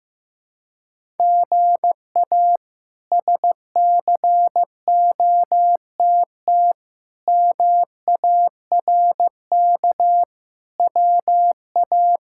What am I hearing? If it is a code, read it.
Morse code, heard as a keyed tone, about 15 wpm, 700 Hz. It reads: GA SCOTT MARK WA
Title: K1USN SST QSO Practice - 15wpm